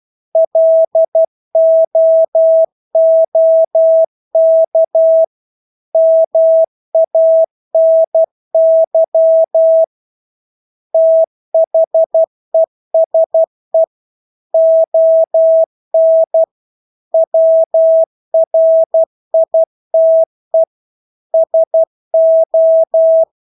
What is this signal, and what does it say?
Morse code, heard as a keyed tone, about 12 wpm, 650 Hz. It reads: LOOK MANY THESE ON WRITE SO
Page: Engelske ord 12 wpm